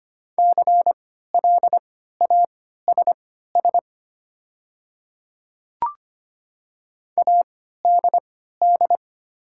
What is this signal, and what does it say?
Morse code, heard as a keyed tone, about 25 wpm, spaced as for 14 wpm, 700 Hz. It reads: CLASS ADD
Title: 3x Character Spacing - Top 500 Words - 25wpm